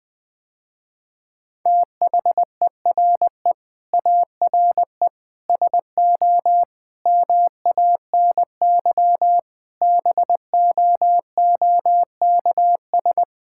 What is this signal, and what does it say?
Morse code, heard as a keyed tone, about 20 wpm, 700 Hz. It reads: THERE ARE SO MANY BOOKS
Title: Sentences with 4-Word Phrases - Spoken Beginning - 20wpm